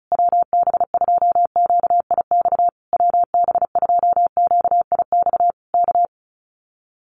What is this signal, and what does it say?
Morse code, heard as a keyed tone, about 34 wpm, 700 Hz. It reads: WB2QIX WB2QIX K
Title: Common QSO Phrases Encore 35wpm